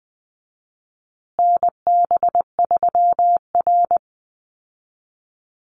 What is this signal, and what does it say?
Morse code, heard as a keyed tone, about 20 wpm, 700 Hz. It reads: NB3R
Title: Call Signs Repeated 2x - 20wpm